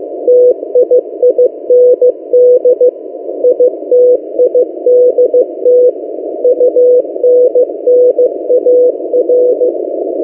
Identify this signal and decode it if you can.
Morse code, heard as a keyed tone, about 15 wpm, 500 Hz. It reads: TIIND ITIDT UNNAR
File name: M08a- -1355utc-8096khz